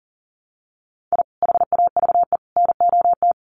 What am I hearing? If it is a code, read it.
Morse code, heard as a keyed tone, about 39 wpm, 700 Hz. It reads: I HAVE NOT
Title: Sets of 3 Words 40wpm